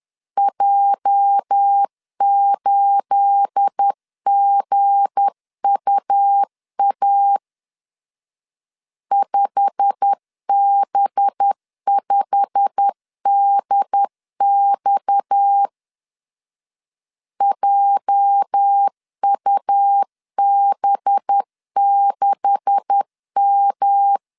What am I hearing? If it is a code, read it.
Morse code, heard as a keyed tone, about 10 wpm, 800 Hz. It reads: J8GUA 5B5DX JUB6M
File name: CW / DET / 1 9 1500C S B 10wpm 4439